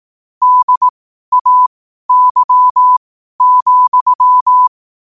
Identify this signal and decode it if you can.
Morse code, heard as a keyed tone, about 18 wpm, spaced as for 12 wpm, 1000 Hz. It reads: DAY,